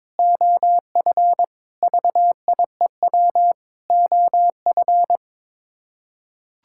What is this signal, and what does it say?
Morse code, heard as a keyed tone, about 21 wpm, 700 Hz. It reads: OF VIEW OF
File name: practice / 22wpm / Sets of 6 Words - 22wpm